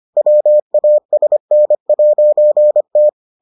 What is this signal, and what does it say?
Morse code, heard as a keyed tone, about 25 wpm, 600 Hz. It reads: WASN'T